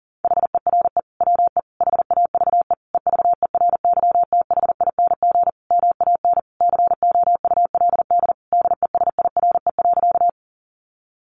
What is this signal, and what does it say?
Morse code, heard as a keyed tone, about 40 wpm, 700 Hz. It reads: HERE WE HAVE EVERYTHING MAN COULD DESIRE.
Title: Sentences from Top 1100 Words 40wpm